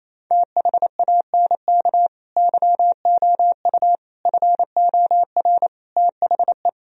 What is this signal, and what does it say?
Morse code, heard as a keyed tone, about 27 wpm, 690 Hz. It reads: THANK YOU FOR THE
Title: Sentences from Top 1000 Words - Review Part 3 - 28wpm